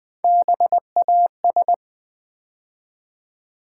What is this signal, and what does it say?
Morse code, heard as a keyed tone, about 20 wpm, 700 Hz. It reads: BAS
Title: Build A Word - 3 Letters At A Time - 6-Letter Words - 20wpm